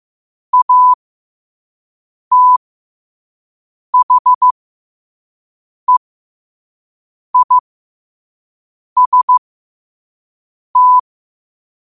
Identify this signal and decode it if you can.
Morse code, heard as a keyed tone, about 15 wpm, spaced as for 5 wpm, 1000 Hz. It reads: ATHEIST
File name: qotdfiles / 5wpm